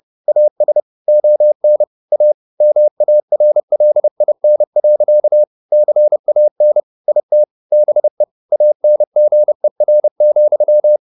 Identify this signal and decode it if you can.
Morse code, heard as a keyed tone, about 30 wpm, 600 Hz. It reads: AS ON A MARLIN. CAN IT BE ANGER,
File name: the old man and the sea 2 30wpm